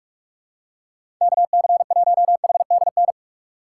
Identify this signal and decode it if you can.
Morse code, heard as a keyed tone, about 46 wpm, 700 Hz. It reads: KC1HDN
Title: USA Call Signs - 6 Characters - 4x Character Spacing - 45wpm